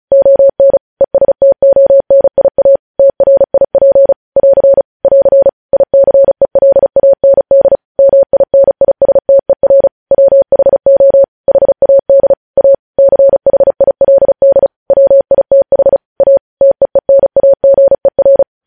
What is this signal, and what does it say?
Morse code, heard as a keyed tone, about 36 wpm, 550 Hz. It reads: ON ESTONIA TRIP <AR> <AR> ICELAND MINISTER WHO HAD A CHILD WITH A TEENAGER
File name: podcast / News headlines-1x-35x35